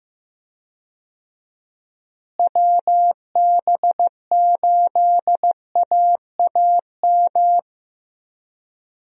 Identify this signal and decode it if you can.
Morse code, heard as a keyed tone, about 15 wpm, 700 Hz. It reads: WB8AAM
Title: USA Call Signs - 6 Characters - Repeated 2x - 15wpm